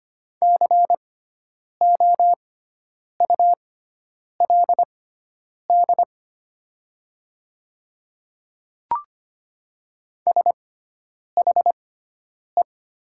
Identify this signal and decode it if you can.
Morse code, heard as a keyed tone, about 25 wpm, spaced as for 9 wpm, 700 Hz. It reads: COULD SHE
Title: 6x Character Spacing - Top 200 Words - 25wpm